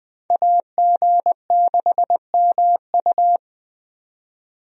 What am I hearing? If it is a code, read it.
Morse code, heard as a keyed tone, about 20 wpm, 700 Hz. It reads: AG6MU
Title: USA Call Signs - 5 Characters - 20wpm